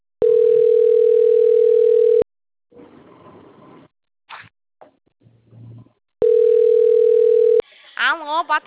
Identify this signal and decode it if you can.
Ringback tone (North America), 440 + 480 Hz, with speech.
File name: out- - - - - .21018